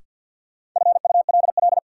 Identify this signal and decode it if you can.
Morse code, heard as a keyed tone, about 48 wpm, 700 Hz. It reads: FULL
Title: Sentences from Top 500 Words - New Word Drill - 50wpm